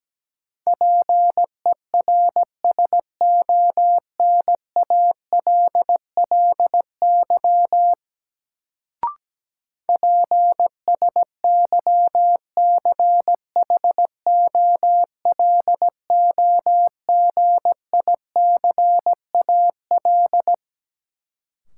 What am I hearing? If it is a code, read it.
Morse code, heard as a keyed tone, about 17 wpm, 700 Hz. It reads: PERSONALLY PSYCHOLOGICAL
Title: Sentences from Top 1400 Words - New Word Drill Alt - 17wpm